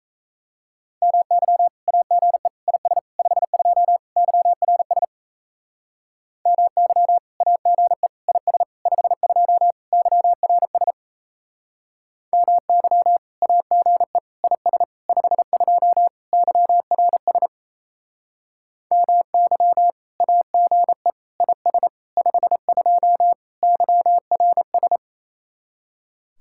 Morse code, heard as a keyed tone, about 36 wpm, 700 Hz. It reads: MY AGE IS 52 YRS MY AGE IS 52 YRS MY AGE IS 52 YRS MY AGE IS 52 YRS